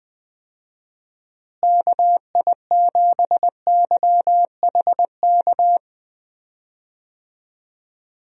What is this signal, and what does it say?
Morse code, heard as a keyed tone, about 20 wpm, 700 Hz. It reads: KI7YHK
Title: USA Call Signs - 6 Characters - Repeated 3x - 20wpm